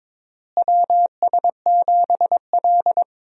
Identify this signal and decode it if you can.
Morse code, heard as a keyed tone, about 21 wpm, 700 Hz. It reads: WS7L
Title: USA Call Signs - 4 Characters - Repeated 2x - 22wpm